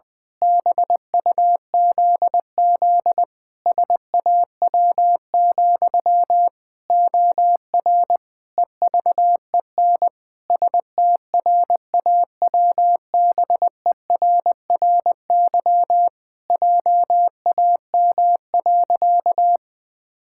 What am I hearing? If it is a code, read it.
Morse code, heard as a keyed tone, about 20 wpm, 700 Hz. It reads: BUZZ SAW, OR EVEN STRAWBERRY JAM.
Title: Pangram Challenge v2 - 20wpm